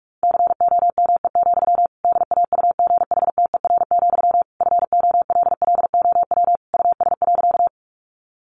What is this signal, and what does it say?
Morse code, heard as a keyed tone, about 44 wpm, 700 Hz. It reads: COME, DAUGHTER, FOLLOW US.
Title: Sentences from Top 1500 Words 45wpm